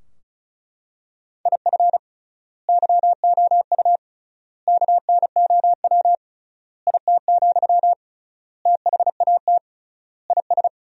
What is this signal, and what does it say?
Morse code, heard as a keyed tone, about 34 wpm, 700 Hz. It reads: IF YOU KNOW IT, THAT IS
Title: 3x Word-Spacing - Sentences from Top 400 Words - 35wpm